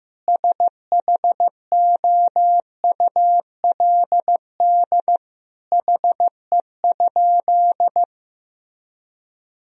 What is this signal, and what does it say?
Morse code, heard as a keyed tone, about 15 wpm, 700 Hz. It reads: SHOULD HE?
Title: Sentences from Top 300 Words and Abbreviations 15wpm